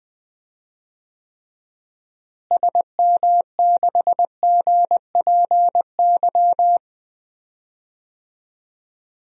Morse code, heard as a keyed tone, about 20 wpm, 700 Hz. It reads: SM6GPY
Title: DXCC-100 Call Signs - 6 Characters - Repeated 3x - 20wpm